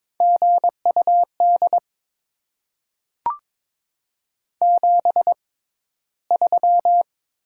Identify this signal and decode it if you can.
Morse code, heard as a keyed tone, about 22 wpm, spaced as for 14 wpm, 700 Hz. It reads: GUD 73
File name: Top 20 QSO Elements - 6x Character Spacing - 22wpm